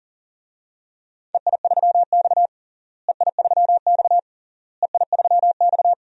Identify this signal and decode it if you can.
Morse code, heard as a keyed tone, about 40 wpm, 700 Hz. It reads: EI3X EI3X EI3X
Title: DXCC-100 Call Signs - 4 Characters - Repeated 3x - 40wpm